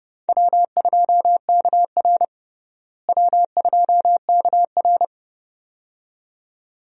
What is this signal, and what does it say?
Morse code, heard as a keyed tone, about 32 wpm, 700 Hz. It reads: W2KR W2KR
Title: USA Call Signs - 4 Characters - Repeated 2x - 30wpm